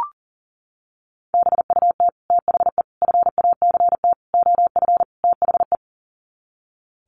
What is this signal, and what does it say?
Morse code, heard as a keyed tone, about 40 wpm, 700 Hz. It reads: BUT THE FACT OF THE
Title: Sets of 5 Words Encore2 40wpm